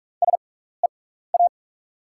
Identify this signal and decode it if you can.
Morse code, heard as a keyed tone, about 44 wpm, spaced as for 15 wpm, 700 Hz. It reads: SEA
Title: 6x Character Spacing - Top 500 Words - 45wpm